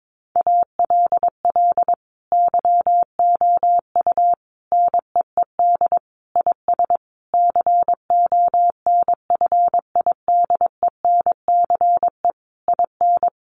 Morse code, heard as a keyed tone, about 22 wpm, 700 Hz. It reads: ALL YOU NEED IS CONFIDENCE IN
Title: Sentences from Top 1500 Words 22wpm